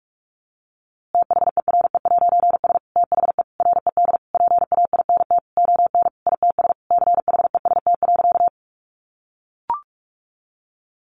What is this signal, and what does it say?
Morse code, heard as a keyed tone, about 44 wpm, 700 Hz. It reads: THERE'S THE RED PAINT ON ITS CHEST.
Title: A Sound Of Thunder 45wpm